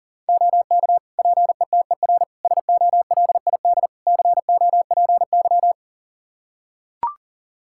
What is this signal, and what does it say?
Morse code, heard as a keyed tone, about 40 wpm, 700 Hz. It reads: OK PETER SOLID COPY
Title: K1USN SST QSO Practice 40wpm